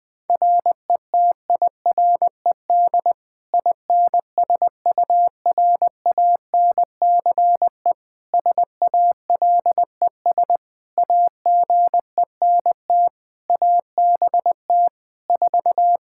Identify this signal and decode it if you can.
Morse code, heard as a keyed tone, about 20 wpm, 700 Hz. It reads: RETIRED INSURANCE SALES AGENT ABT 4
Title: K1USN SST QSO Practice 20wpm